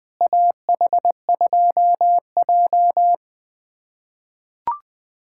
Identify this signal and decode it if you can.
Morse code, heard as a keyed tone, about 20 wpm, 700 Hz. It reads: AH2J
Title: USA Callsign DE Callsign - 4 Characters - 20wpm